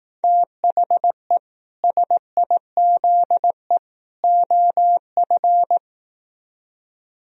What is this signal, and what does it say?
Morse code, heard as a keyed tone, about 18 wpm, 700 Hz. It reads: THE SIZE OF
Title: Speed-Racing - Sets of 3 Words 15wpm